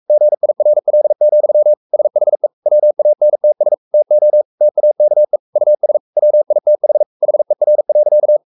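Morse code, heard as a keyed tone, about 44 wpm, 600 Hz. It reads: GIRL, SHE WANTS TO TAKE US WITH HER.
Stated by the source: Morse code at 43 wpm sending Ken Follett's novel Never in English, from CW med Gnister.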